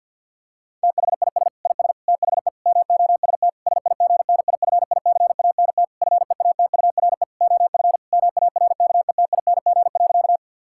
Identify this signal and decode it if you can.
Morse code, heard as a keyed tone, about 50 wpm, 700 Hz. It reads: THIS IS THE MOST SIGNIFICANT FEATURE OF MARKETING.
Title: Sentences from Top 1000 Words - Review Part 6 - 50wpm